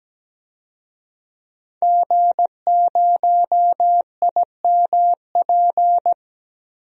Morse code, heard as a keyed tone, about 17 wpm, 700 Hz. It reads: G0IMP
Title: DXCC-100 Call Signs - 5 Characters - Repeated 2x - 17wpm